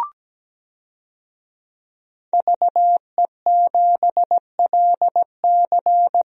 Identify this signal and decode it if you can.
Morse code, heard as a keyed tone, about 17 wpm, 700 Hz. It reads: VE7LC
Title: DXCC-100 Call Signs - 5 Characters - 17wpm